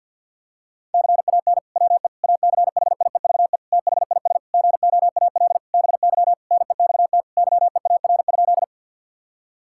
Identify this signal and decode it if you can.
Morse code, heard as a keyed tone, about 50 wpm, 700 Hz. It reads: CAN WE ACHIEVE THIS GOAL BY NEXT YEAR?
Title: Sentences from Top 1000 Words - Review Part 5 - 50wpm